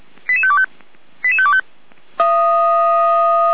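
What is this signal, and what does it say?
Five-tone selective call (ZVEI), heard as quick runs of five tones, tones of about 70 ms. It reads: 88525 88525
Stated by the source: Five-tone ZVEI selective call received on an SDR.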